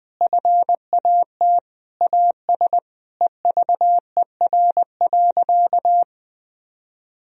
Morse code, heard as a keyed tone, about 21 wpm, 700 Hz. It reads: FAT AS EVER.
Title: Sentences from Top 1000 Words - Review Part 4 - 20wpm